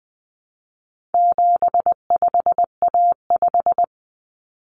Morse code, heard as a keyed tone, about 20 wpm, 700 Hz. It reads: 75A5
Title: Common Cut Numbers - 4-Digit 20wpm